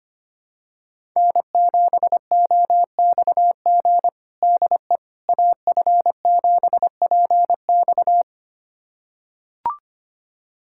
Morse code, heard as a keyed tone, about 25 wpm, 700 Hz. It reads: N7OXG DE AF7PX
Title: USA Callsign DE Callsign - 5 Characters - 25wpm